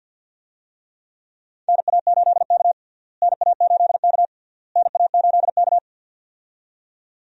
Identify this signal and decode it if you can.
Morse code, heard as a keyed tone, about 51 wpm, 700 Hz. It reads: NA8K NA8K NA8K
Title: USA Call Signs - 4 Characters - Repeated 3x - 50wpm